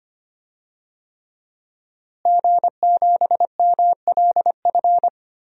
Morse code, heard as a keyed tone, about 25 wpm, 700 Hz. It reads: G7MLF
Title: DXCC-100 Call Signs - 5 Characters - Repeated 3x - 25wpm